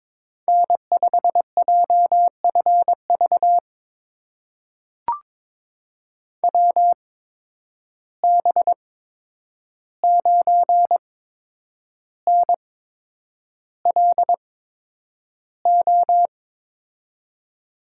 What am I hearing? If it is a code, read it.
Morse code, heard as a keyed tone, about 22 wpm, spaced as for 9 wpm, 700 Hz. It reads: N5JFV WB9NLO
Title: USA Call Signs - 8x Character Spacing - 22wpm